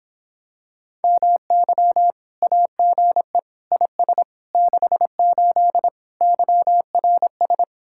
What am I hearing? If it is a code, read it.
Morse code, heard as a keyed tone, about 26 wpm, 700 Hz. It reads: MY AGE IS 68 YRS